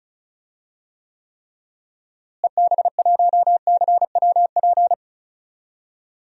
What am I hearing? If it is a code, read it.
Morse code, heard as a keyed tone, about 34 wpm, 700 Hz. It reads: EB1CWP
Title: DXCC-100 Call Signs - 6 Characters - Repeated 3x - 35wpm